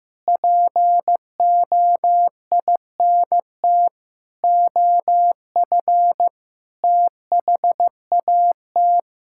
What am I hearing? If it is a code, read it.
Morse code, heard as a keyed tone, about 16 wpm, 700 Hz. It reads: POINT OF THAT
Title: Sentences from Top 1000 Words - Review Part 5 - 15wpm